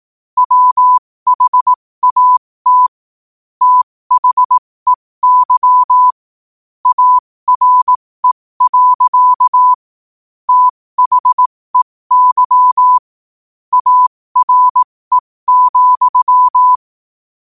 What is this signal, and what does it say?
Morse code, heard as a keyed tone, about 18 wpm, spaced as for 15 wpm, 1000 Hz. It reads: WHAT THEY ARE. THEY ARE,
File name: qotdfiles / 15wpm